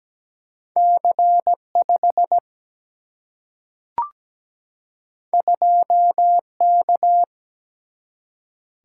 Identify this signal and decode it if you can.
Morse code, heard as a keyed tone, about 17 wpm, 700 Hz. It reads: C 5 2 K
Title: Single-Letter Single-Number 17wpm